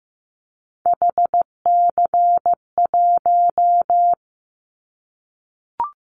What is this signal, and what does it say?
Morse code, heard as a keyed tone, about 15 wpm, 700 Hz. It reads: HC1
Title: Callsign Prefixes for Top 100 DXCC - 15wpm